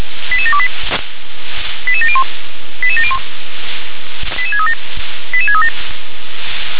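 Five-tone selective call (ZVEI), heard as quick runs of five tones, tones of about 70 ms. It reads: 88728 88701 88701 88637 88637
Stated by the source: ZVEI five-tone selective call received on SDR.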